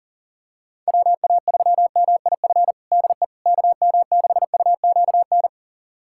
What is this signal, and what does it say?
Morse code, heard as a keyed tone, about 37 wpm, 700 Hz. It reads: WA3MIF DE KM6UQN
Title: USA Callsign DE Callsign - 6 Characters - 40wpm